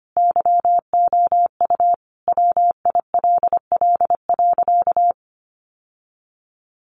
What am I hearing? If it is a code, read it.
Morse code, heard as a keyed tone, about 25 wpm, 700 Hz. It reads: YOU WILL.
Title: Sentences from Top 1400 Words 25wpm